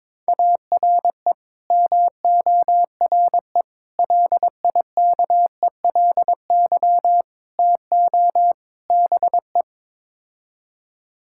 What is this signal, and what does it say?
Morse code, heard as a keyed tone, about 22 wpm, 700 Hz. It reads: ARE MORE LIKELY TO BE
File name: practice / 22wpm / Build A Phrase - 5-Word Phrases - 22wpm